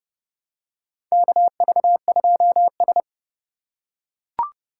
Morse code, heard as a keyed tone, about 30 wpm, 700 Hz. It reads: KV2S
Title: USA Call Signs - 4 Characters - 2x Character Spacing - 30wpm